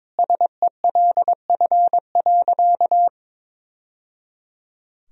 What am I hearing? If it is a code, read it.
Morse code, heard as a keyed tone, about 22 wpm, 700 Hz. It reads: SELF.
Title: Sentences from Top 1000 Words - Review Part 1 - 22wpm